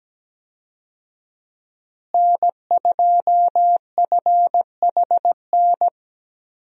Morse code, heard as a keyed tone, about 17 wpm, 700 Hz. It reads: N2FHN
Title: USA Call Signs - 5 Characters - Repeated 2x - 17wpm